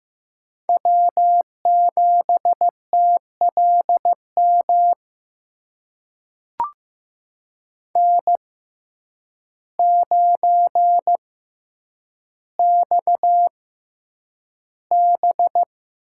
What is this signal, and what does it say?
Morse code, heard as a keyed tone, about 15 wpm, spaced as for 8 wpm, 700 Hz. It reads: W7TLM N9XB
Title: USA Call Signs - 5 Characters - 6x Character Spacing - 15wpm